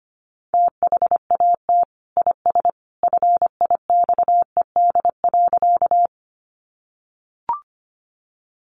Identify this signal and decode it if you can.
Morse code, heard as a keyed tone, about 25 wpm, 700 Hz. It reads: THAT IS FIXED.
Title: Sentences from Top 1100 Words 25wpm